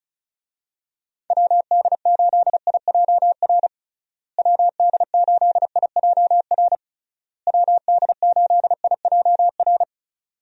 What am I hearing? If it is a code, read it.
Morse code, heard as a keyed tone, about 34 wpm, 700 Hz. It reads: WD8IJR WD8IJR WD8IJR
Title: USA Call Signs - 6 Characters - Repeated 3x - 35wpm